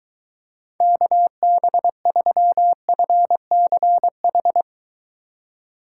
Morse code, heard as a keyed tone, about 24 wpm, 700 Hz. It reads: KB3FCH